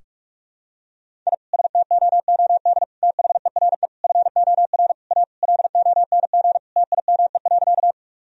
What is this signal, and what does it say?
Morse code, heard as a keyed tone, about 46 wpm, 700 Hz. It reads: I STOOD THERE FOR A LONG TIME.
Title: Sentences from Top 1000 Words - Review Part 2 - 45wpm